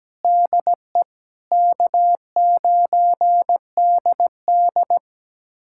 Morse code, heard as a keyed tone, about 17 wpm, 700 Hz. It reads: DE K9DD
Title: USA Callsign DE Callsign - 4 Characters - 17wpm